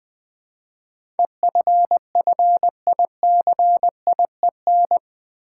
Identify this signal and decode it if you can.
Morse code, heard as a keyed tone, about 20 wpm, 700 Hz. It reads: EFFICIEN
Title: Build A Word - 4 Letters At A Time - 11-Letter Words - 20wpm